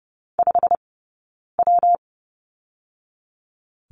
Morse code, heard as a keyed tone, about 30 wpm, spaced as for 9 wpm, 700 Hz. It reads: 5W
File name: Lesson 20d - Number 3 - Two Characters 30wpm